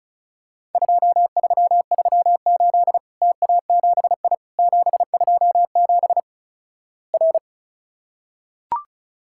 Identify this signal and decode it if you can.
Morse code, heard as a keyed tone, about 36 wpm, 700 Hz. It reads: 2338 TA7I 727 R
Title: Worked All Europe DX Contest - QTC Exchange - 35wpm